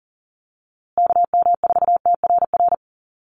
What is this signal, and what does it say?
Morse code, heard as a keyed tone, about 40 wpm, 700 Hz. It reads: KM4TRR
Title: USA Call Signs Repeated 2x 40wpm